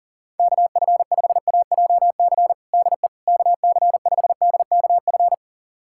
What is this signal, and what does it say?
Morse code, heard as a keyed tone, about 40 wpm, 700 Hz. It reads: KF5AJC DE KC5DKF
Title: USA Callsign DE Callsign - 6 Characters - 40wpm